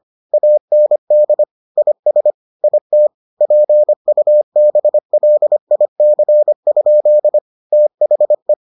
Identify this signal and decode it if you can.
Morse code, heard as a keyed tone, about 25 wpm, 600 Hz. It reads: AND IS IT PUBLIC? THE